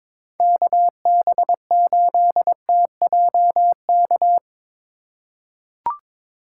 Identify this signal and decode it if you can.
Morse code, heard as a keyed tone, about 22 wpm, 700 Hz. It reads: KB8TJK
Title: USA Call Signs - 6 Characters - 4x Character Spacing - 22wpm